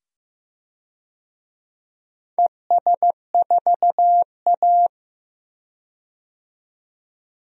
Morse code, heard as a keyed tone, about 15 wpm, 700 Hz. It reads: ES4A
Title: DXCC-100 Call Signs - 4 Characters - 15wpm